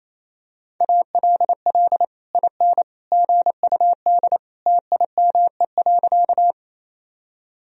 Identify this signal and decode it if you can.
Morse code, heard as a keyed tone, about 28 wpm, 700 Hz. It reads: ALL IN GUD TIME.
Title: Sentences from Top 200 Words and Abbreviations 28wpm